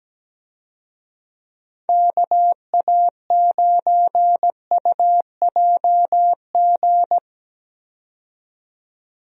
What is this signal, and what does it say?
Morse code, heard as a keyed tone, about 17 wpm, 700 Hz. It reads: KA9UJG
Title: USA Call Signs - 6 Characters - 17wpm